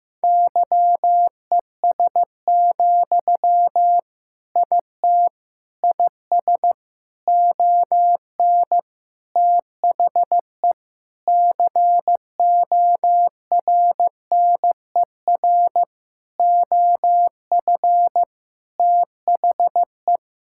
Morse code, heard as a keyed tone, about 15 wpm, 700 Hz. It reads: YES, IT IS ON THE CORNER OF THE
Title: Sentences from Top 1600 Words 15wpm